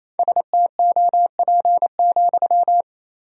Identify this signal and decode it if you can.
Morse code, heard as a keyed tone, about 28 wpm, 700 Hz. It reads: STOP,
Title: 2x Word-Spacing - Sentences from Top 300 Words - 28wpm